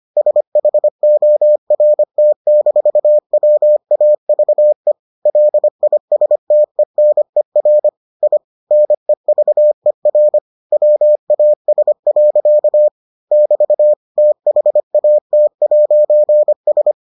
Morse code, heard as a keyed tone, about 25 wpm, 600 Hz. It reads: SHORT-WAVE LISTENER I NEVER WAS. <BT> THAT'S